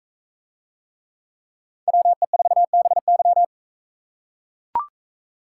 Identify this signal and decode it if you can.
Morse code, heard as a keyed tone, about 41 wpm, 700 Hz. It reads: WE4BY